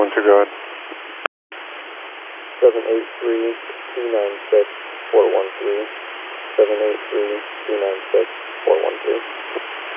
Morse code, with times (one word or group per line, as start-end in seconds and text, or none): none